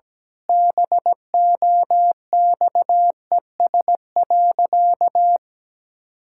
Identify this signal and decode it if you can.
Morse code, heard as a keyed tone, about 17 wpm, 700 Hz. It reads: BOXES.
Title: Pangram Challenge v2 - 17wpm